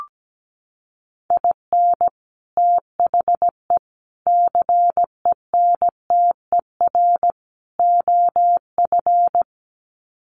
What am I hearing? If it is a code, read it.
Morse code, heard as a keyed tone, about 17 wpm, 700 Hz. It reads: IN THE CENTER OF
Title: Sets of 4 Words 17wpm